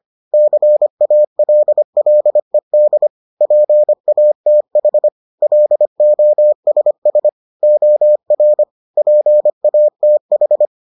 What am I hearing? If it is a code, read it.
Morse code, heard as a keyed tone, about 25 wpm, 600 Hz. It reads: CALLED PATH LOSS OR PATH